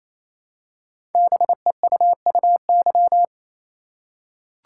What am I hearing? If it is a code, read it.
Morse code, heard as a keyed tone, about 28 wpm, 700 Hz. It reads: BEUUY
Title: Code Groups - 5 Letters - 28wpm